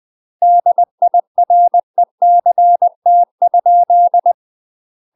Morse code, heard as a keyed tone, about 20 wpm, 700 Hz. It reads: DIRECT?
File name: UK MASTER QSOs TRAINING / 20WPM / UK MASTER QSOs 0017